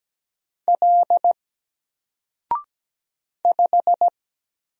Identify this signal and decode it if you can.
Morse code, heard as a keyed tone, about 17 wpm, 700 Hz. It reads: L 5
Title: Single Character (Letter:Number:Punctuation:Prosign) 17wpm